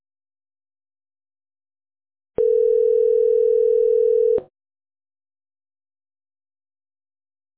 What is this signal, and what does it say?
Ringback tone (North America), 440 + 480 Hz.